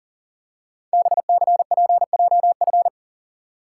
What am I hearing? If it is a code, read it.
Morse code, heard as a keyed tone, about 38 wpm, 700 Hz. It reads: BCPJF
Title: Code Groups - 5 Letters - 40wpm